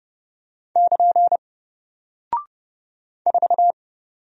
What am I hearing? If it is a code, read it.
Morse code, heard as a keyed tone, about 30 wpm, 700 Hz. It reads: <KN> 4
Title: Single Character (Letter:Number:Punctuation:Prosign) 30wpm